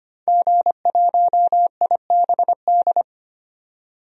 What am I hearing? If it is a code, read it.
Morse code, heard as a keyed tone, about 25 wpm, 700 Hz. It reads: G1IBD
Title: DXCC-100 Call Signs - 5 Characters - 25wpm